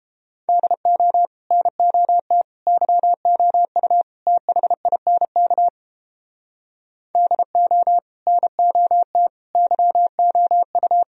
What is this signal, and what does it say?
Morse code, heard as a keyed tone, about 29 wpm, 700 Hz. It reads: DO NOT YOU THINK DO NOT YOU